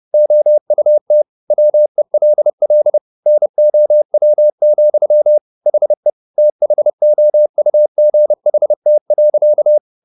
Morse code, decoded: OUT WELL NOW, HE THOUGHT.

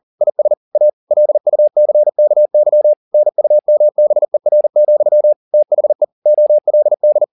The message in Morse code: IS A LUCKY NUMBER, THE OLD

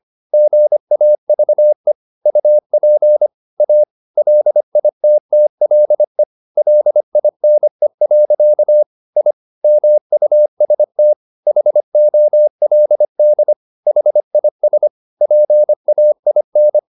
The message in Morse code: GAVE UP A LITTLE LINE. I MUST HOLD HIS PAIN

